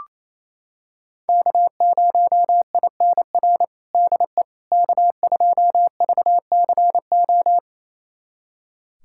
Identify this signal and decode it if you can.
Morse code, heard as a keyed tone, about 28 wpm, 700 Hz. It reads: K0INR DE K2VCO
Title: USA Callsign DE Callsign - 5 Characters - 28wpm